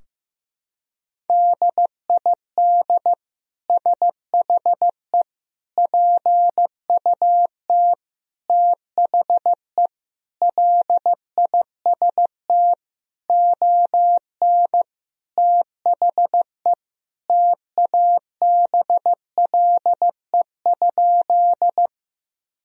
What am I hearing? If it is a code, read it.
Morse code, heard as a keyed tone, about 15 wpm, 700 Hz. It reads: DID SHE PUT THE LIST ON THE TABLE?
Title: Sentences from Top 1000 Words - Review Part 2 - 15wpm